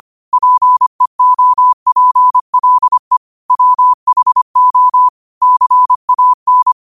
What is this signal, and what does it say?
Morse code, heard as a keyed tone, about 25 wpm, 1000 Hz. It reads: PEOPLE WHO CAN